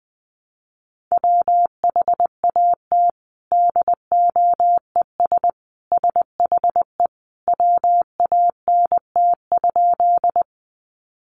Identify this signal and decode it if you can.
Morse code, heard as a keyed tone, about 20 wpm, 700 Hz. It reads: WHAT DOES SHE WANT?
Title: Sentences from Top 200 Words 20wpm